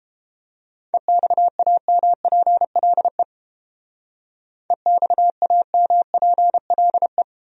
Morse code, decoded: EXAMPLE EXAMPLE

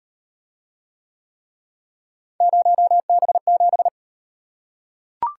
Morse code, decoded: 0B7

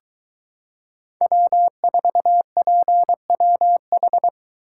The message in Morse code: W4PWH

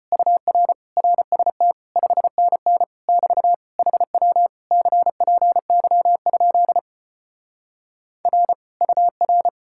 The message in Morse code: UR RST 5NN <BT> HW CPY? R UR

